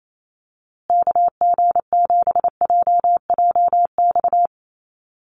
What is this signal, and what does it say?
Morse code, heard as a keyed tone, about 28 wpm, 700 Hz. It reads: KG7JJX
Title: USA Call Signs Repeated 2x 28wpm